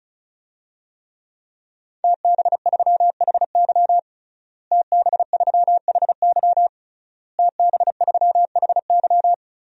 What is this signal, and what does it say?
Morse code, heard as a keyed tone, about 36 wpm, 700 Hz. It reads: TB3HY TB3HY TB3HY
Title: DXCC-100 Call Signs - 5 Characters - Repeated 3x - 35wpm